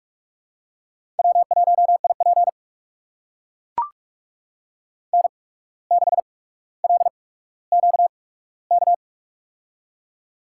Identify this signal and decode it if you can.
Morse code, heard as a keyed tone, about 44 wpm, spaced as for 18 wpm, 700 Hz. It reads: W1IP N6LQK
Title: USA Call Signs - 8x Character Spacing - 45wpm